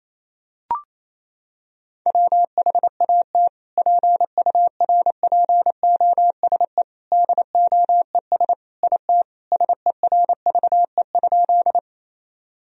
Morse code, heard as a keyed tone, about 28 wpm, 700 Hz. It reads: WHAT PURPOSE DOES IT SERVE?
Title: Sentences from Top 1000 Words - Review Part 2 - 28wpm